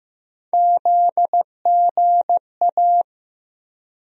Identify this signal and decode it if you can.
Morse code, heard as a keyed tone, about 15 wpm, 700 Hz. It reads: ZGA
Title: USA Call Sign Postfixes - 3 Letters 15wpm